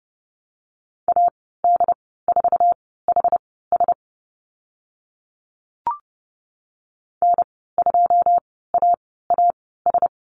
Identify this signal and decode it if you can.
Morse code, heard as a keyed tone, about 30 wpm, spaced as for 17 wpm, 700 Hz. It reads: AD4HS N2AAS
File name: Lesson 15g - Number 5 - Call Signs 30wpm